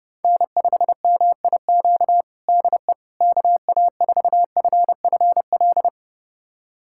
Morse code, heard as a keyed tone, about 30 wpm, 700 Hz. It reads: N5MIQ DE KA4FFL
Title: USA Callsign DE Callsign 30wpm